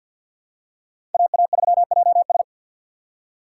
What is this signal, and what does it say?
Morse code, heard as a keyed tone, about 51 wpm, 700 Hz. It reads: AA3JS
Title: USA Call Signs - 5 Characters - 3x Character Spacing - 50wpm